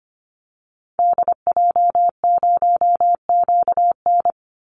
Morse code, heard as a keyed tone, about 25 wpm, 700 Hz. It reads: DJ0QN
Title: Call Signs Repeated 2x - 25wpm